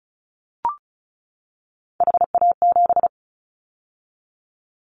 Morse code, heard as a keyed tone, about 34 wpm, 700 Hz. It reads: HA7
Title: Callsign Prefixes for Top 25 DXCC - 35wpm